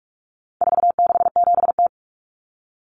Morse code, heard as a keyed tone, about 44 wpm, 700 Hz. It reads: 467T
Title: Common Cut Numbers - 4-Digit 45wpm